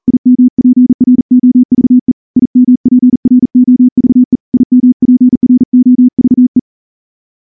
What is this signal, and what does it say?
Morse code, heard as a keyed tone, about 38 wpm, 250 Hz. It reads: IMPROVE IMPROVE IMPROVE
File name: morse / morse-training-40WPM